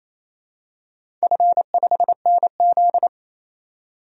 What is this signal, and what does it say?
Morse code, heard as a keyed tone, about 28 wpm, 700 Hz. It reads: F5NZ
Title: DXCC-100 Call Signs - 4 Characters - 28wpm